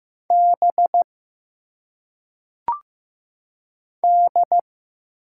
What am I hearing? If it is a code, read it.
Morse code, heard as a keyed tone, about 15 wpm, 700 Hz. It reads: B D
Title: Single Letters - Phonetic Pronunciation 15wpm